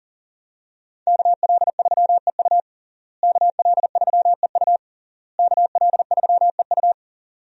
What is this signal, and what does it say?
Morse code, heard as a keyed tone, about 37 wpm, 700 Hz. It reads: KL3EU KL3EU KL3EU
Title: USA Call Signs - 5 Characters - Repeated 3x - 40wpm